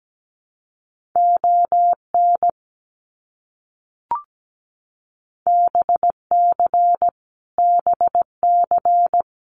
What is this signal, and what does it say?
Morse code, heard as a keyed tone, about 17 wpm, 700 Hz. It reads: ON BC BC
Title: Canadian Provinces and Territories - Repeated 3x - 17wpm